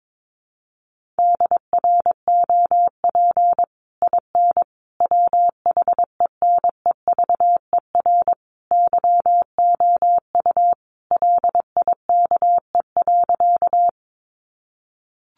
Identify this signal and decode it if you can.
Morse code, heard as a keyed tone, about 22 wpm, 700 Hz. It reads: DROP IN WHENEVER YOU LIKE.
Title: Sentences from Top 1400 Words 22wpm